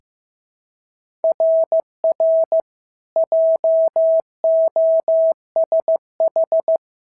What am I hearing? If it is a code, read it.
Morse code, heard as a keyed tone, about 14 wpm, 650 Hz. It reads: RR JOSH
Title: SOTA QSO Practice 15wpm